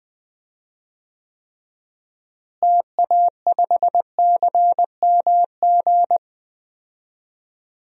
Morse code, heard as a keyed tone, about 20 wpm, 700 Hz. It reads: TA5CMG
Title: DXCC-100 Call Signs - 6 Characters - Repeated 2x - 20wpm